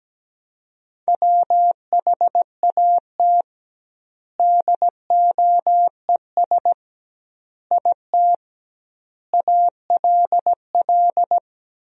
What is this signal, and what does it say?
Morse code, heard as a keyed tone, about 17 wpm, 700 Hz. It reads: WHAT DOES IT ALL
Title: 2x Word-Spacing - Sentences from Top 200 Words - 17wpm